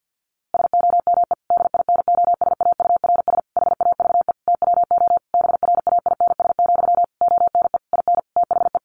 Morse code, heard as a keyed tone, about 50 wpm, 700 Hz. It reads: SOME DINOSAURS HAVE TWO BRAINS, ONE IN THE
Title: A Sound Of Thunder 50wpm